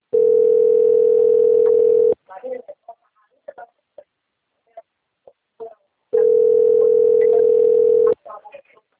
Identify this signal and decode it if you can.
Ringback tone (North America), 440 + 480 Hz, with speech.
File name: out- - - - - .83873